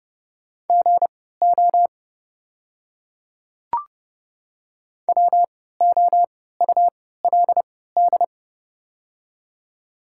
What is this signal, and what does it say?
Morse code, heard as a keyed tone, about 30 wpm, spaced as for 17 wpm, 690 Hz. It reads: GO WOULD
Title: 3x Character Spacing - Top 100 Words - 30wpm